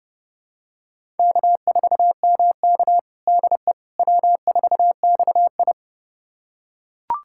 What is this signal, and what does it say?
Morse code, heard as a keyed tone, about 30 wpm, 700 Hz. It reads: K4MK DE W4XI
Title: USA Callsign DE Callsign - 4 Characters - 30wpm